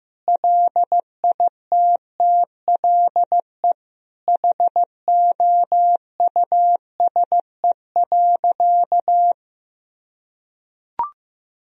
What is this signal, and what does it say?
Morse code, heard as a keyed tone, about 15 wpm, 700 Hz. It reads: LITTLE HOUSE.
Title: Sentences from Top 200 Words and Abbreviations 15wpm